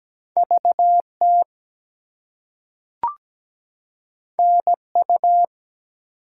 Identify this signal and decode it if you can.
Morse code, heard as a keyed tone, about 17 wpm, 700 Hz. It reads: VT NU
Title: US States and Canadian Provinces 17wpm